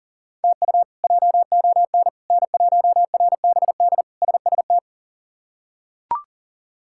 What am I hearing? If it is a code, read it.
Morse code, heard as a keyed tone, about 37 wpm, 700 Hz. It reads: TU JON N1RBD SST
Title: K1USN SST QSO Practice - 40wpm